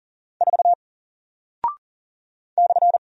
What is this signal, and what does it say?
Morse code, heard as a keyed tone, about 39 wpm, 700 Hz. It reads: 4 /
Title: Single Character (Letter:Number:Punctuation:Prosign) Alt 40wpm